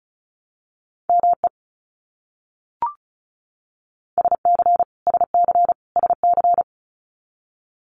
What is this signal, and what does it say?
Morse code, heard as a keyed tone, about 34 wpm, 700 Hz. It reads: ME SC SC SC
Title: US States Repeated 3x 35wpm